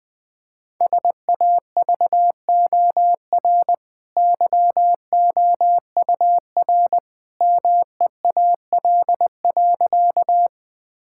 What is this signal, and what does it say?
Morse code, heard as a keyed tone, about 20 wpm, 700 Hz. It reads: SAVOR YOUR MEAL.
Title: Easy Sentences with Hints - 20wpm